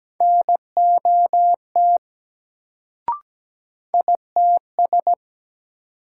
Morse code, heard as a keyed tone, about 17 wpm, 700 Hz. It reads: NOT ITS
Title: Sentences from Top 600 Words - New Word Drill - 17wpm